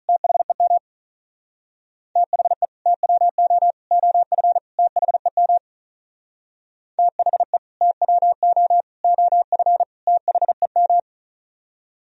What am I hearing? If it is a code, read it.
Morse code, heard as a keyed tone, about 38 wpm, 700 Hz. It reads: THEM THE TWO OF THEM THE TWO OF THEM
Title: Speed-Racing - Sets of 4 Words 35wpm